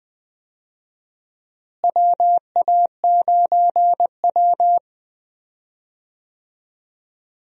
Morse code, heard as a keyed tone, about 20 wpm, 700 Hz. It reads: WA9W
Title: USA Call Signs - 4 Characters - Repeated 3x - 20wpm